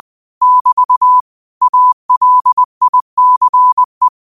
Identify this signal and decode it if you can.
Morse code, heard as a keyed tone, about 20 wpm, 1000 Hz. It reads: <BT> ALICE